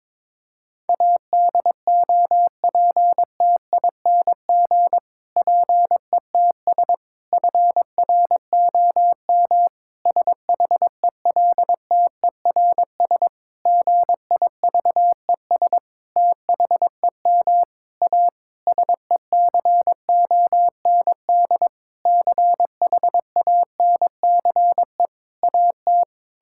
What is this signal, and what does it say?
Morse code, heard as a keyed tone, about 22 wpm, 700 Hz. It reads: ADOPTING PETS FROM SHELTERS GIVES THEM A SECOND CHANCE AT
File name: Difficult Sentences with Hints - 22wpm